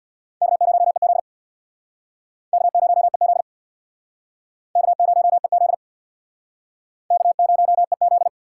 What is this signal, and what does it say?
Morse code, heard as a keyed tone, about 68 wpm, 700 Hz. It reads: K0EZ K0EZ K0EZ K0EZ